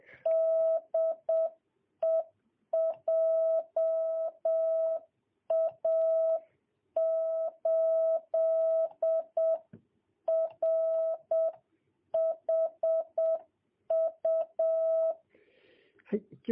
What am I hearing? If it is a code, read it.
Morse code, heard as a keyed tone, about 7 wpm, 660 Hz. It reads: DE JA8RHU
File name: jh8cbh / 2022cw